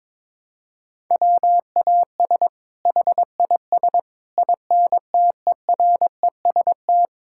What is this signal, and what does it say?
Morse code, heard as a keyed tone, about 22 wpm, 700 Hz. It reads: WAS HIS INTEREST
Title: Sentences from Top 1000 Words - Review Part 5 - 22wpm